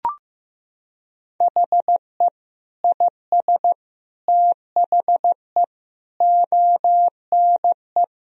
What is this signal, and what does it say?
Morse code, heard as a keyed tone, about 15 wpm, 700 Hz. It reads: HE IS THE ONE
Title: Sentences from Top 1000 Words - Review Part 3 - 15wpm